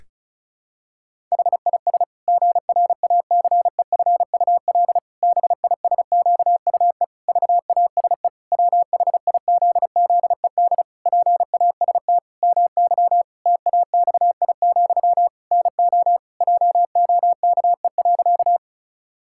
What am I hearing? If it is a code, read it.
Morse code, heard as a keyed tone, about 34 wpm, 700 Hz. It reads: HIS GRACEFUL BISQUE VASE WHIZZED PAST MY TAXI, NO JOKE.
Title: Pangram Challenge v2 - 35wpm